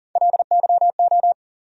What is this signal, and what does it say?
Morse code, heard as a keyed tone, about 40 wpm, 700 Hz. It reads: LYO
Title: USA Call Sign Postfixes - 3 Letters 40wpm